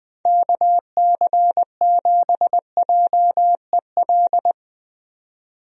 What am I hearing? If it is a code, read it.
Morse code, heard as a keyed tone, about 20 wpm, 700 Hz. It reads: KC7JEL
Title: USA Call Signs - 6 Characters - 20wpm